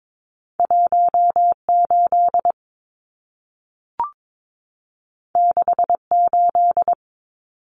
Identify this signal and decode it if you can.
Morse code, heard as a keyed tone, about 22 wpm, 700 Hz. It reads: 18 68
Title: Numbers - 2-Digit 22wpm